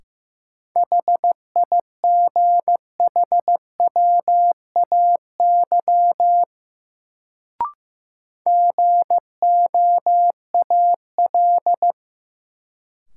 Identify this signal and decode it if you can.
Morse code, heard as a keyed tone, about 15 wpm, 700 Hz. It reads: HIGHWAY GOAL
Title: Sentences from Top 1400 Words - New Word Drill Alt - 15wpm